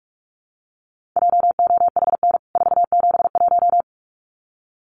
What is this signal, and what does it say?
Morse code, heard as a keyed tone, about 44 wpm, 700 Hz. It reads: JOHN 471
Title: CWT Exchange 45wpm